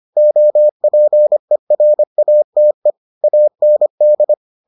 Morse code, heard as a keyed tone, about 25 wpm, 600 Hz. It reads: OPERATE AND